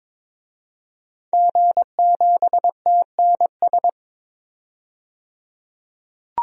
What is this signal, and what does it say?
Morse code, heard as a keyed tone, about 21 wpm, 700 Hz. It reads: G7TNS
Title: DXCC-100 Call Signs - 5 Characters - 22wpm